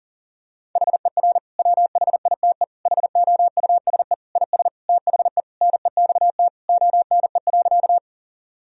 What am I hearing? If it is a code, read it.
Morse code, heard as a keyed tone, about 40 wpm, 700 Hz. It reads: HER WHITE HOUSE IS THE NEXT ONE.
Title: Sentences from Top 1000 Words - Review Part 1 - 40wpm